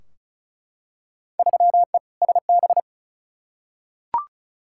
Morse code, heard as a keyed tone, about 34 wpm, 700 Hz. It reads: 3E SB
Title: Field Day Exchanges Encore - 35wpm